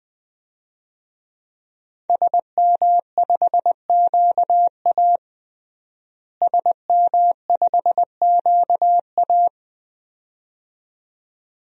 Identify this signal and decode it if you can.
Morse code, heard as a keyed tone, about 20 wpm, 700 Hz. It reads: SM5QA SM5QA
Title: DXCC-100 Call Signs - 5 Characters - Repeated 2x - 20wpm